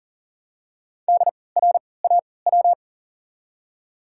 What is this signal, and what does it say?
Morse code, heard as a keyed tone, about 40 wpm, spaced as for 23 wpm, 700 Hz. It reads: DRAW
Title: 3x Character Spacing - Top 300 Words - 40wpm